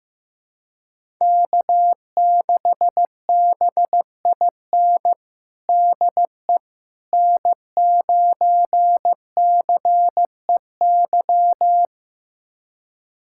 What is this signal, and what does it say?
Morse code, heard as a keyed tone, about 15 wpm, 700 Hz. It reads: K6BIN DE N9CEY
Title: USA Callsign DE Callsign - 5 Characters - 15wpm